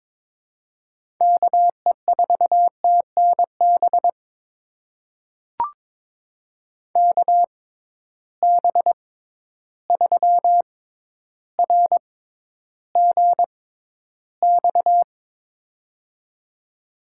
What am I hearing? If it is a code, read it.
Morse code, heard as a keyed tone, about 22 wpm, spaced as for 11 wpm, 700 Hz. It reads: KE4TNB KB3RGX
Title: USA Call Signs - 6 Characters - 6x Character Spacing - 22wpm